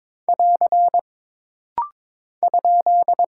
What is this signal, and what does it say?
Morse code, heard as a keyed tone, about 22 wpm, 700 Hz. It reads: <AR> ?
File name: practice / 22wpm / Single Punctuation-Prosign - Rapid-Fire - 22wpm